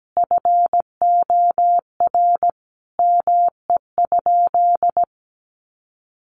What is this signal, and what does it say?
Morse code, heard as a keyed tone, about 17 wpm, 700 Hz. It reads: FOR ME?
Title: Sentences from Top 700 Words 17wpm